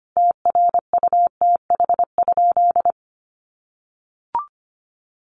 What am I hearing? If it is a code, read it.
Morse code, heard as a keyed tone, about 25 wpm, 700 Hz. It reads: TRUTH?
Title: Sentences from Top 1000 Words 25wpm